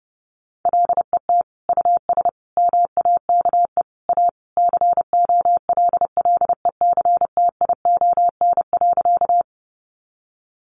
Morse code, heard as a keyed tone, about 30 wpm, 700 Hz. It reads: LET US MAKE A COLLECTION.
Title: Sentences from Top 1200 Words 30wpm